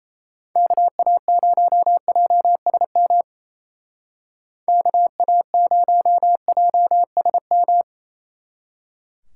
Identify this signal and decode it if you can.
Morse code, heard as a keyed tone, about 34 wpm, 700 Hz. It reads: KA0JSM KA0JSM
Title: Speed Racing - USA 2x3 Call Signs 28wpm